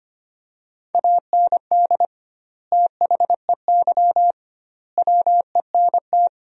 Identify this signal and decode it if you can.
Morse code, heard as a keyed tone, about 25 wpm, 700 Hz. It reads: AND THEY WENT
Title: 2x Word-Spacing - Sentences from Top 200 Words - 25wpm